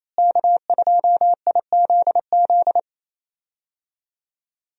Morse code, heard as a keyed tone, about 28 wpm, 700 Hz. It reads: K2IZZ
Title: USA Call Signs - 5 Characters - Repeated 2x - 28wpm